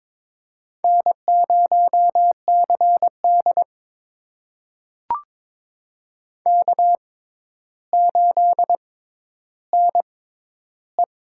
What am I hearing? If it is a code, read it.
Morse code, heard as a keyed tone, about 22 wpm, spaced as for 11 wpm, 700 Hz. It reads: N0CD K8NE
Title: USA Call Signs - 4 Characters - 6x Character Spacing - 22wpm